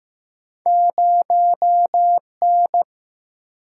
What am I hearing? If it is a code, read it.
Morse code, heard as a keyed tone, about 15 wpm, 700 Hz. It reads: 0 N
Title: Single-Letter Single-Number 15wpm